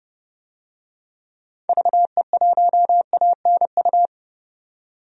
Morse code, heard as a keyed tone, about 30 wpm, 700 Hz. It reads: VE1ANU
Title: DXCC-100 Call Signs - 6 Characters - 30wpm